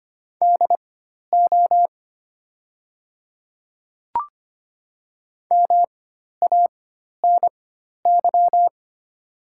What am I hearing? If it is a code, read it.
Morse code, heard as a keyed tone, about 25 wpm, spaced as for 12 wpm, 700 Hz. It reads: DO MANY
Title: 4x Character Spacing - Top 200 Words - 25wpm